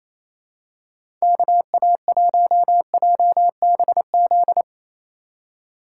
Morse code, heard as a keyed tone, about 28 wpm, 700 Hz. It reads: KA1JBZ
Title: USA Call Signs - 4x Character Spacing - 28wpm